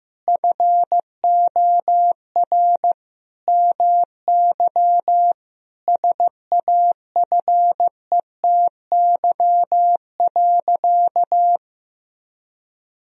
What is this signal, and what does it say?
Morse code, heard as a keyed tone, about 15 wpm, 700 Hz. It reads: FOR MY SAFETY.
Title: Sentences from Top 1000 Words - Review Part 4 - 15wpm